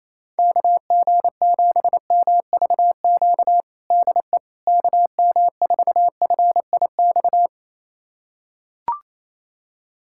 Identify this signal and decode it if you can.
Morse code, heard as a keyed tone, about 28 wpm, 700 Hz. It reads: KG7MVQ DE KM4FIX
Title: USA Callsign DE Callsign 28wpm